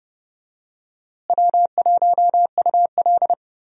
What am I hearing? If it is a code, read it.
Morse code, heard as a keyed tone, about 30 wpm, 700 Hz. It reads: W1UL
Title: USA Call Signs - 4 Characters - Repeated 3x - 30wpm